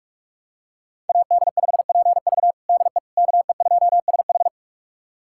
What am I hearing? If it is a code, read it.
Morse code, heard as a keyed tone, about 43 wpm, 700 Hz. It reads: AD5PV DE KE2SH
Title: USA Callsign DE Callsign - 5 Characters - 45wpm